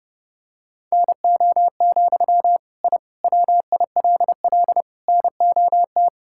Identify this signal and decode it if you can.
Morse code, heard as a keyed tone, about 30 wpm, 700 Hz. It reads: NO, I WILL NOT
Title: Sentences from Top 1000 Words - Review Part 5 - 30wpm